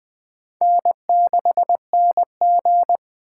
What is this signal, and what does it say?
Morse code, heard as a keyed tone, about 20 wpm, 700 Hz. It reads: N6NG
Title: Speed Racing - USA 1x2 Call Signs 17wpm